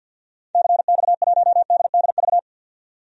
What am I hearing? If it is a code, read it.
Morse code, heard as a keyed tone, about 50 wpm, 700 Hz. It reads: CX1DDV
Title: DXCC-100 Call Signs - 6 Characters - 50wpm